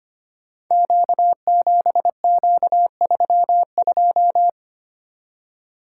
Morse code, heard as a keyed tone, about 25 wpm, 700 Hz. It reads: Q7Q32
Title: Code Groups - 5 Letter:Numbers - 25wpm